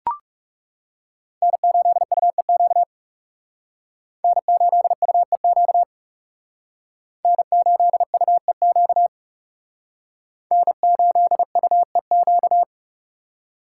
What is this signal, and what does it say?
Morse code, heard as a keyed tone, about 36 wpm, 700 Hz. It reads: N8UEQ N8UEQ N8UEQ N8UEQ
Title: Speed Racing - USA 1x3 Call Signs 30wpm